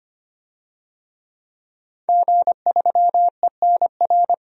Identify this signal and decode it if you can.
Morse code, heard as a keyed tone, about 25 wpm, 700 Hz. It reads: G3ENR
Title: DXCC-100 Call Signs - 5 Characters - 25wpm